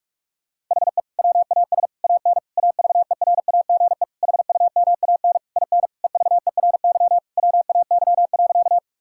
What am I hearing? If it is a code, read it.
Morse code, heard as a keyed tone, about 46 wpm, 700 Hz. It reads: HE WAS AN AVERAGE HUMAN IN EVERY WAY.
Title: Sentences from Top 1000 Words - Review Part 6 - 45wpm